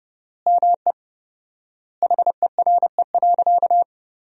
Morse code, heard as a keyed tone, about 30 wpm, 700 Hz. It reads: ME HERE.
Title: 4x Word-Spacing - Sentences from Top 200 Words - 30wpm